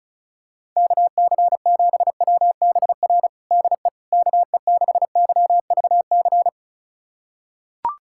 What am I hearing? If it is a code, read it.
Morse code, heard as a keyed tone, about 36 wpm, 700 Hz. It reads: KC7WBR DE KE6YVC
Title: USA Callsign DE Callsign - 6 Characters - 35wpm